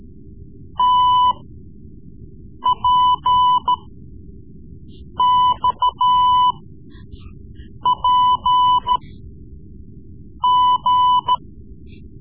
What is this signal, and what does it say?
Morse code, heard as a keyed tone, about 14 wpm, spaced as for 5 wpm, 1000 Hz. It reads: TPXPG
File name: Morse Lessons / Morse Lessons / Tape-3B